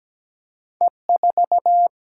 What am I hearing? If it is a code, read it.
Morse code, heard as a keyed tone, about 17 wpm, 700 Hz. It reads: E4
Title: Single-Letter Single-Number 17wpm